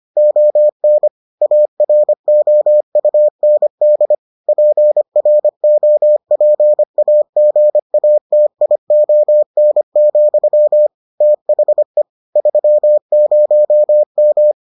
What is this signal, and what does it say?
Morse code, heard as a keyed tone, about 25 wpm, 600 Hz. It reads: ON AROUND PROPAGATION, THE 30M